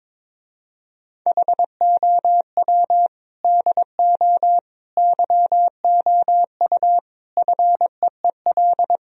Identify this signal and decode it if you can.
Morse code, heard as a keyed tone, about 22 wpm, 700 Hz. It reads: HOW DO YOU FEEL
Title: Speed-Racing - Sets of 5 Words 22wpm